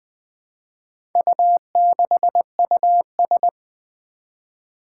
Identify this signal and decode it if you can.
Morse code, heard as a keyed tone, about 20 wpm, 700 Hz. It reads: U6US
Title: Four Challenging Letter-Number Combinations 20wpm